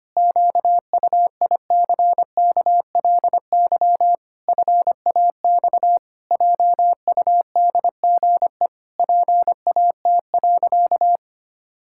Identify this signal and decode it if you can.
Morse code, heard as a keyed tone, about 25 wpm, 700 Hz. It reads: QUICKLY FAX JUDGE PAT.
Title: Pangram Challenge v2 - 25wpm